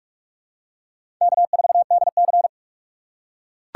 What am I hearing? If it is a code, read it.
Morse code, heard as a keyed tone, about 43 wpm, 700 Hz. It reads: K4DC
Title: USA Call Signs - 4 Characters - 2x Character Spacing - 45wpm